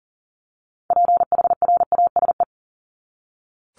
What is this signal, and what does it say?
Morse code, heard as a keyed tone, about 40 wpm, 700 Hz. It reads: PHRASE
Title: Top 1501-1750 Words 40wpm